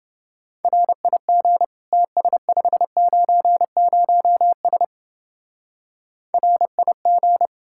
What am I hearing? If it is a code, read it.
Morse code, heard as a keyed tone, about 31 wpm, 700 Hz. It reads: RIG TS590S RIG